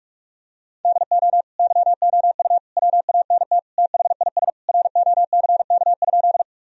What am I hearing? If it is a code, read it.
Morse code, heard as a keyed tone, about 41 wpm, 700 Hz. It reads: DO YOU WANT THIS ROCK?
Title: Sentences from Top 1000 Words - Review Part 2 - 45wpm